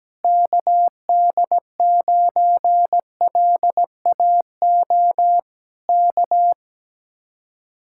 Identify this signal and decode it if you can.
Morse code, heard as a keyed tone, about 17 wpm, 700 Hz. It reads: KD9LAO K
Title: Speed-Racing Common QSO Phrases - 17wpm